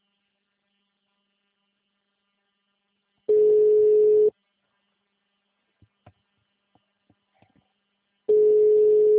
Ringback tone (Europe, 425 Hz).
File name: out- -2004- - - .78908